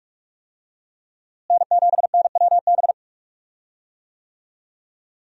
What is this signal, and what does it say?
Morse code, heard as a keyed tone, about 47 wpm, 700 Hz. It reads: N7NWB
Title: USA Call Signs - 5 Characters - Repeated 2x - 45wpm